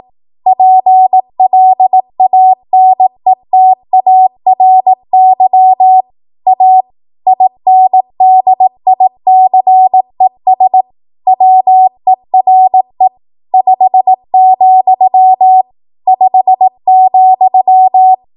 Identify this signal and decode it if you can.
Morse code, heard as a keyed tone, about 18 wpm, 750 Hz. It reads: PLANETARY A INDICES WERE 5, 5,
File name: Archive / 18 WPM / 18WPM